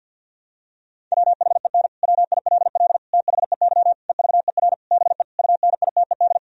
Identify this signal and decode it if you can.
Morse code, heard as a keyed tone, about 51 wpm, 700 Hz. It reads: WHEN WILL THEY EVER BE UNITED
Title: Sentences from Top 1000 Words - Review Part 4 - 50wpm